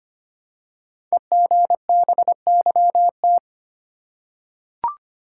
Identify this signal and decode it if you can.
Morse code, heard as a keyed tone, about 25 wpm, 700 Hz. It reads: EGBYT
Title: Code Groups - 5 Letter:Numbers - 25wpm